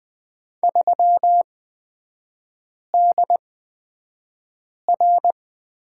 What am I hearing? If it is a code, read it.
Morse code, heard as a keyed tone, about 20 wpm, spaced as for 5 wpm, 690 Hz. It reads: 3DR
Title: 255 Lesson 41e - Period - Three Characters 20wpm